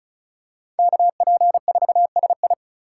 Morse code, heard as a keyed tone, about 36 wpm, 700 Hz. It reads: KP4SI
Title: USA Call Signs - 5 Characters - 3x Character Spacing - 35wpm